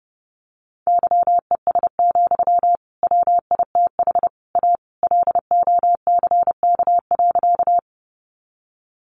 Morse code, heard as a keyed tone, about 30 wpm, 700 Hz. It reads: YES, WITH A LOCK.
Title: Sentences from Top 1300 Words 30wpm